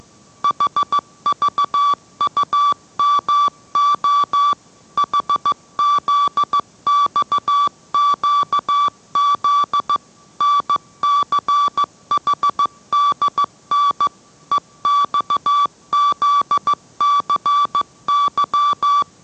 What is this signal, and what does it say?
Morse code, heard as a keyed tone, about 15 wpm, 1160 Hz. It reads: HVUMO HZXQZ NCHDN EXZCY